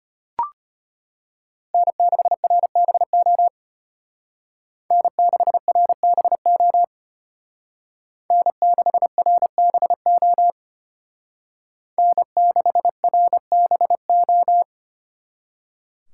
Morse code, heard as a keyed tone, about 32 wpm, 700 Hz. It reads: N6RBO N6RBO N6RBO N6RBO